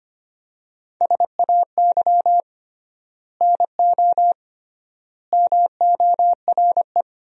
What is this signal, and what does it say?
Morse code, heard as a keyed tone, about 25 wpm, 700 Hz. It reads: SAY NO MORE
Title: 3x Word-Spacing - Sentences from Top 200 Words - 25wpm